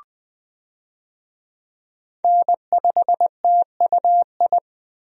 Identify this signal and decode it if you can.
Morse code, heard as a keyed tone, about 20 wpm, 700 Hz. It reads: N5TUI
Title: USA Call Signs - 5 Characters - 20wpm